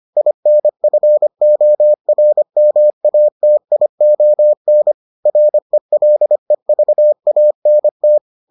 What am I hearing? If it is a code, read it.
Morse code, heard as a keyed tone, about 25 wpm, 600 Hz. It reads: INFORMATION RELEVANT